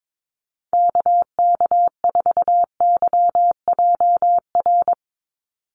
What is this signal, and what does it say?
Morse code, heard as a keyed tone, about 22 wpm, 700 Hz. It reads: KK4YJR
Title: USA Call Signs 22wpm